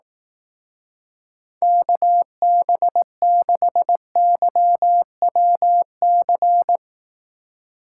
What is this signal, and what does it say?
Morse code, heard as a keyed tone, about 18 wpm, 700 Hz. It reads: KB6YWC
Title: Speed Racing - USA 2x3 Call Signs 15wpm